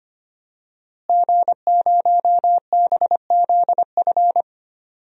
Morse code, heard as a keyed tone, about 25 wpm, 700 Hz. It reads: G0BZF
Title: DXCC-100 Call Signs - 5 Characters - Repeated 2x - 25wpm